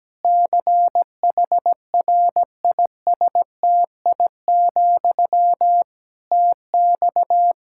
Morse code, heard as a keyed tone, about 17 wpm, 700 Hz. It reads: CHRISTI, TX
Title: Practice QSO - Exchange 1 - Beginning - 17wpm